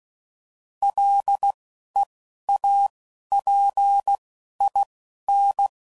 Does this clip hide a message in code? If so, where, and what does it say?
Yes: Morse code from 0.8 to 5.7 s, heard as a keyed tone, about 16 wpm, 800 Hz. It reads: L E A P I N